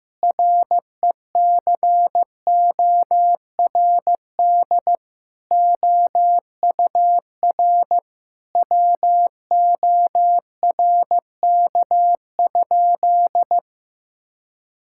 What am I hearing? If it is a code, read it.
Morse code, heard as a keyed tone, about 15 wpm, 700 Hz. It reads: RECORD OUR WORK?
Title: Sentences from Top 1000 Words - Review Part 6 - 15wpm